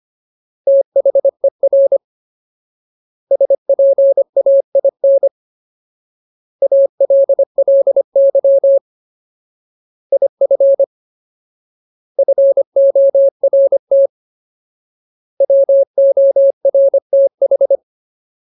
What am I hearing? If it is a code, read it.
Morse code, heard as a keyed tone, about 25 wpm, 550 Hz. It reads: THER SPAIN ALLY IF FORT WORTH
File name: Ultimate Word List / 25WPM Ultimate